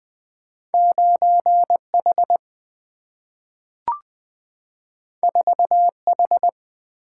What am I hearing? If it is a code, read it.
Morse code, heard as a keyed tone, about 20 wpm, 700 Hz. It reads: 9H 4H
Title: Single-Letter Single-Number 20wpm